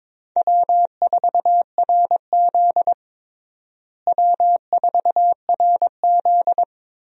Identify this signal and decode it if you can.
Morse code, heard as a keyed tone, about 22 wpm, 700 Hz. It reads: W4RZ W4RZ
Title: USA Call Signs - 4 Characters - Repeated 3x - 22wpm